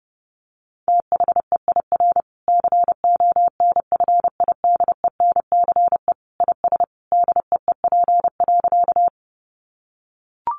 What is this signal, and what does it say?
Morse code, heard as a keyed tone, about 30 wpm, 700 Hz. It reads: THEIR CONFIDENCE IS DEEP.
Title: Sentences from Top 1500 Words 30wpm